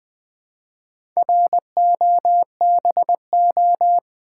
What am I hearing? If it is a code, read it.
Morse code, heard as a keyed tone, about 20 wpm, 700 Hz. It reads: ROBO
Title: Build A Word - 1 Letter At A Time - 5-Letter Words - 20wpm